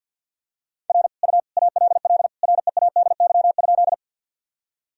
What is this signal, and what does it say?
Morse code, heard as a keyed tone, about 52 wpm, 700 Hz. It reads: R U ALL READY?
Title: Sentences from Top 400 Words and Abbreviations 50wpm